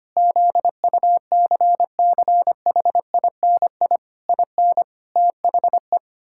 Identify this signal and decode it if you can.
Morse code, heard as a keyed tone, about 25 wpm, 700 Hz. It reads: ZUCCHINI IN THE
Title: Pangram Challenge v2 - 25wpm